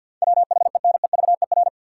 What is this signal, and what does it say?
Morse code, heard as a keyed tone, about 47 wpm, 700 Hz. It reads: WHENEVER
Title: Sentences from Top 1200 Words - New Word Drill Alt - 50wpm